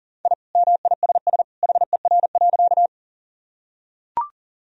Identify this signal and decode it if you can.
Morse code, heard as a keyed tone, about 40 wpm, 700 Hz. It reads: I MISS HER.
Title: Sentences from Top 1000 Words - Review Part 3 - 40wpm